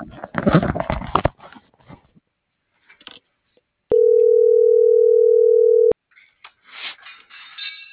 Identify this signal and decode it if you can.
Ringback tone (North America), 440 + 480 Hz.